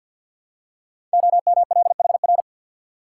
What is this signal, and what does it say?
Morse code, heard as a keyed tone, about 48 wpm, 700 Hz. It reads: OMLHR
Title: Code Groups - 5 Letters - 50wpm